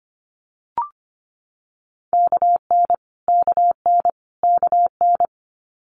Morse code, heard as a keyed tone, about 25 wpm, 700 Hz. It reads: KN KN KN
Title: Multi-letter Phonemes Repeated 3x 25wpm